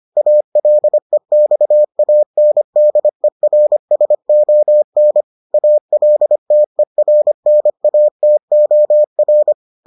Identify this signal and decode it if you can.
Morse code, heard as a keyed tone, about 25 wpm, 600 Hz. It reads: ALEXANDERSON ALTERNATOR